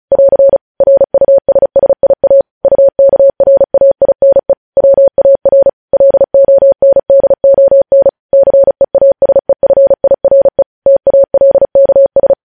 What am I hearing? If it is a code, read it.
Morse code, heard as a keyed tone, about 36 wpm, 550 Hz. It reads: <AR> RUSSIA UKRAINE WAR LONDON CEASEFIRE TALKS
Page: news headlines in morse code at 35 wpm